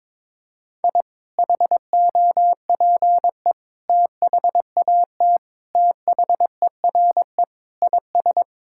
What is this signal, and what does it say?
Morse code, heard as a keyed tone, about 22 wpm, 700 Hz. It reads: I HOPE THAT THERE IS
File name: practice / 22wpm / Sentences with 2-Word Phrases - 22wpm